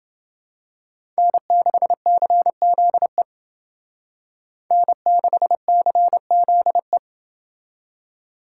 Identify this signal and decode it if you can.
Morse code, heard as a keyed tone, about 28 wpm, 700 Hz. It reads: N6CZE N6CZE